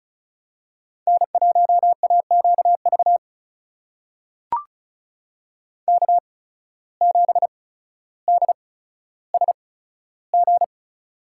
Morse code, heard as a keyed tone, about 34 wpm, spaced as for 15 wpm, 700 Hz. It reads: N1AQV K7DSG
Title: USA Call Signs - 5 Characters - 8x Character Spacing - 35wpm